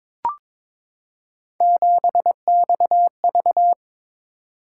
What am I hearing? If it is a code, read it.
Morse code, heard as a keyed tone, about 22 wpm, 700 Hz. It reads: 7XV
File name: practice / 22wpm / Three Letter-Number Combinations - 22wpm